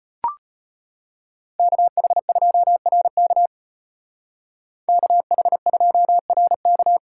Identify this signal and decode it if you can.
Morse code, heard as a keyed tone, about 30 wpm, 700 Hz. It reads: KH2RK KH2RK
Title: Speed Racing - USA 2x2 Call Signs 25wpm